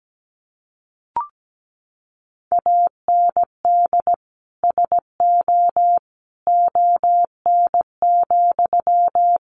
Morse code, heard as a keyed tone, about 17 wpm, 700 Hz. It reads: AND SO ON,
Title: Sentences from Top 100 Words 17wpm